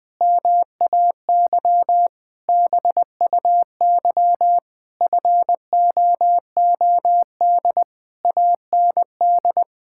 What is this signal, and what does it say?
Morse code, heard as a keyed tone, about 20 wpm, 700 Hz. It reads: MAY BUY FOOD AND
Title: Sentences from Top 1000 Words - Review Part 5 - 20wpm